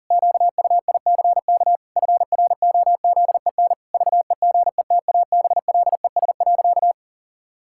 Morse code, heard as a keyed tone, about 37 wpm, 700 Hz. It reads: QUICK FROZEN VEGETABLES.
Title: Pangram Challenge v2 - 40wpm